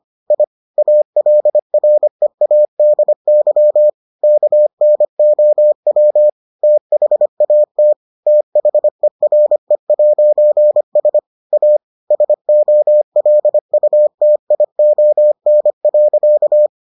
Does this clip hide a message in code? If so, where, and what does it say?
Yes: Morse code from 0.3 to 16.7 s, heard as a keyed tone, about 25 wpm, 600 Hz. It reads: I ALREADY KNOW THAT THERE'S A SOLUTION.